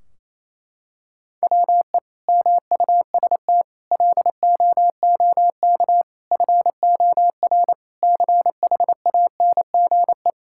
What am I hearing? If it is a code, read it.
Morse code, heard as a keyed tone, about 28 wpm, 700 Hz. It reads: WE MUST LOOK FOR CHANGE
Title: Sentences from Top 1600 Words 28wpm